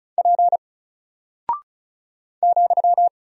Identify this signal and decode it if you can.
Morse code, heard as a keyed tone, about 36 wpm, 700 Hz. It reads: P ,
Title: Single Character (Letter:Number:Punctuation:Prosign) Alt 35wpm